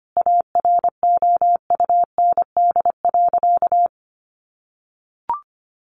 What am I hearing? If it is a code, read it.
Morse code, heard as a keyed tone, about 25 wpm, 700 Hz. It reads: AROUND.
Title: Sentences from Top 600 Words 25wpm